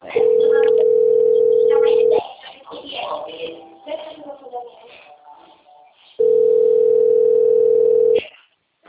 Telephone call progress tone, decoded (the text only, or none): ringback tone